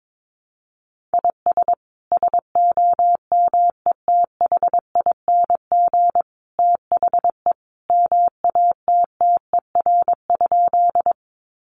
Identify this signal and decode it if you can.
Morse code, heard as a keyed tone, about 22 wpm, 700 Hz. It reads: IS SOMETHING THE MATTER?
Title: Sentences from Top 700 Words 22wpm